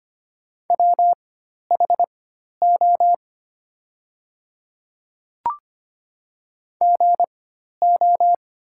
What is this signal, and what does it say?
Morse code, heard as a keyed tone, about 25 wpm, spaced as for 12 wpm, 700 Hz. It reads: WHO GO
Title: 4x Character Spacing - Top 100 Words - 25wpm